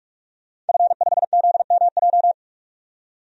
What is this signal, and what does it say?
Morse code, heard as a keyed tone, about 45 wpm, 700 Hz. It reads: F5ZMJ
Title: DXCC-100 Call Signs - 5 Characters - 45wpm